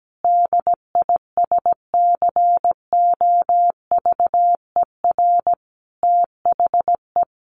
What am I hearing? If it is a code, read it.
Morse code, heard as a keyed tone, about 17 wpm, 700 Hz. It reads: DISCOVER THE
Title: Sentences from Top 1200 Words 17wpm